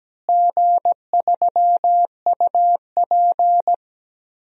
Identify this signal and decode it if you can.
Morse code, heard as a keyed tone, about 17 wpm, 700 Hz. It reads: G3UP
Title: DXCC-100 Call Signs - 4 Characters - Repeated 2x - 17wpm